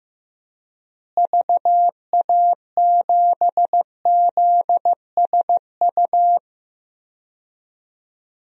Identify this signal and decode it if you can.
Morse code, heard as a keyed tone, about 16 wpm, 700 Hz. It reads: VA7ZSU
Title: DXCC-100 Call Signs - 6 Characters - Repeated 2x - 15wpm